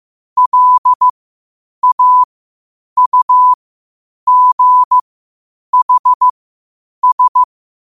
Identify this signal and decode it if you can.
Morse code, heard as a keyed tone, about 15 wpm, spaced as for 8 wpm, 1000 Hz. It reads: LAUGHS
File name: qotdfiles / 8wpm